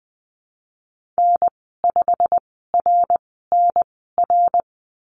Morse code, heard as a keyed tone, about 20 wpm, spaced as for 15 wpm, 700 Hz. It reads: N5RNR
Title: 122 Lesson 20g - Number 3 - Call Signs 20wpm